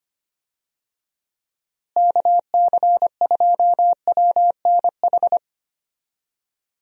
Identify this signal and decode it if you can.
Morse code, heard as a keyed tone, about 25 wpm, 700 Hz. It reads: KC2WNH
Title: USA Call Signs - 6 Characters - Repeated 3x - 25wpm